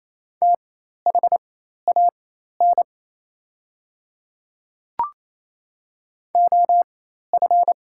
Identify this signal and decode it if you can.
Morse code, heard as a keyed tone, about 28 wpm, spaced as for 13 wpm, 700 Hz. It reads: THAN OF
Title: 4x Character Spacing - Top 100 Words - 28wpm